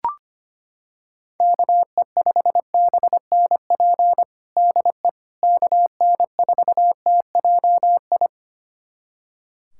Morse code, heard as a keyed tone, about 25 wpm, 700 Hz. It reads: KE5BNP DE KN4TJI
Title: USA Callsign DE Callsign 25wpm